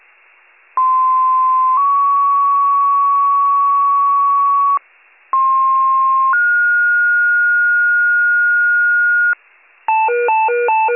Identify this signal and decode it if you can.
Two-tone page, heard as a one-second tone, then a three-second tone, about 1040 then 1120 Hz, about 1060 then 1430 Hz. It is heard from 0.8 to 9.3 s.